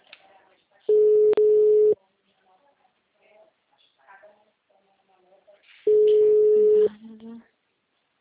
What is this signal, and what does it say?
Ringback tone (Europe, 425 Hz), with speech.